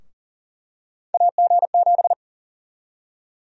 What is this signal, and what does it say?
Morse code, heard as a keyed tone, about 40 wpm, 700 Hz. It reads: AG7
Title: USA Call Sign Prefixes - 40wpm